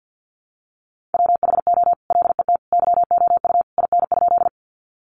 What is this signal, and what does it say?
Morse code, heard as a keyed tone, about 50 wpm, 700 Hz. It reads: WHO LET YOU IN?
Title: Sentences from Top 300 Words 50wpm